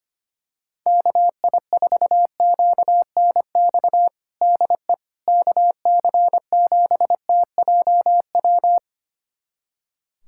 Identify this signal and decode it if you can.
Morse code, heard as a keyed tone, about 25 wpm, 700 Hz. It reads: KI4QNX DE KC7TJW
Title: USA Callsign DE Callsign - 6 Characters - 25wpm